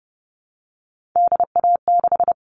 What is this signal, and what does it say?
Morse code, heard as a keyed tone, about 30 wpm, 700 Hz. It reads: DA6
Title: All Cut Numbers - 3-Digit 30wpm